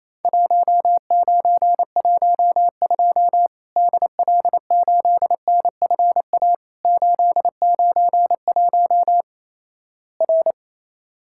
Morse code, heard as a keyed tone, about 27 wpm, 700 Hz. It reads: 1912 DL8NFA 891 R
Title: Worked All Europe DX Contest - QTC Exchange - 28wpm